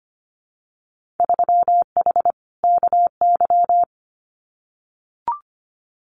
Morse code, heard as a keyed tone, about 25 wpm, 700 Hz. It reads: 3H KY
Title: Winter Field Day Exchanges Encore 25wpm